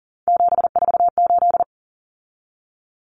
Morse code, heard as a keyed tone, about 40 wpm, 700 Hz. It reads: 748
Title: Common Cut Numbers - 3-Digit 40wpm